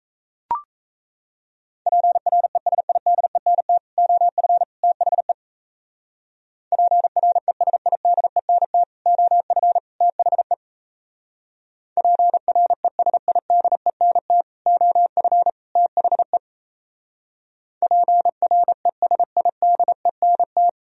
Morse code, heard as a keyed tone, about 36 wpm, 700 Hz. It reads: PRESIDENT OF THE PRESIDENT OF THE PRESIDENT OF THE PRESIDENT